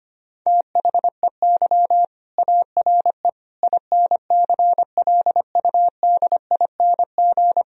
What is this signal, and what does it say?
Morse code, heard as a keyed tone, about 25 wpm, 700 Hz. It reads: THEY ARE INCLUDING
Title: Sentences from Top 1000 Words - Review Part 4 - 25wpm